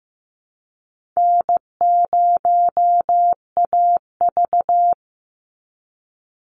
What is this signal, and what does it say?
Morse code, heard as a keyed tone, about 15 wpm, 700 Hz. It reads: N0AV
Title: Easy Call Signs Repeated 2x - 15wpm